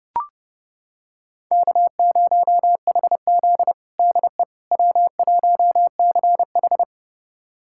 Morse code, heard as a keyed tone, about 30 wpm, 700 Hz. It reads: K0HZ DE W1CH
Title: USA Callsign DE Callsign - 4 Characters - 30wpm